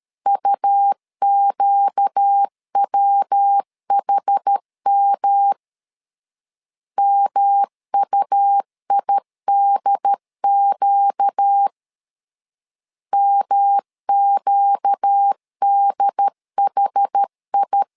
Morse code, heard as a keyed tone, about 13 wpm, 800 Hz. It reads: UQWHM MUIDQ MQDHI
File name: LEZIONI / 4 Q 12wpm 1833